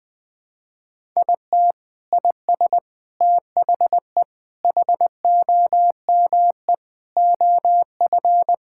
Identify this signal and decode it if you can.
Morse code, heard as a keyed tone, about 20 wpm, 700 Hz. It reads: IT IS THE HOME OF
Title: Sentences from Top 1000 Words - Review Part 5 - 20wpm